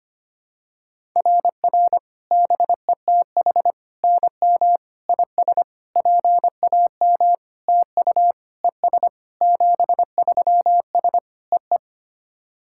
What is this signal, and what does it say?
Morse code, heard as a keyed tone, about 25 wpm, 700 Hz. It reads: RR BETH NM IS PAM TU ES 73S EE
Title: SOTA QSO Practice 25wpm